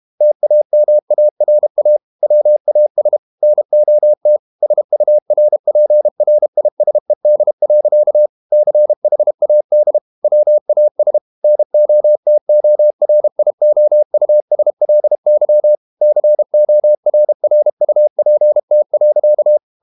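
Morse code, heard as a keyed tone, about 32 wpm, 600 Hz. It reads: TAMARA WAS NOT SURPRISED. CHAD WAS NOTORIOUSLY CORRUPT.
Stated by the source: Morse code practice at 32 wpm from CW med Gnister, reading Ken Follett's novel Never in English.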